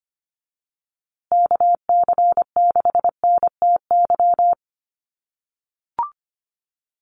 Morse code, heard as a keyed tone, about 25 wpm, 700 Hz. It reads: KC6NTY
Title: USA Call Signs Repeated 2x 25wpm